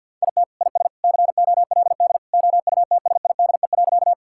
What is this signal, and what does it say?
Morse code, heard as a keyed tone, about 52 wpm, 700 Hz. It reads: IT IS COLD OUTSIDE.
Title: Sentences from Top 1000 Words - Review Part 6 - 50wpm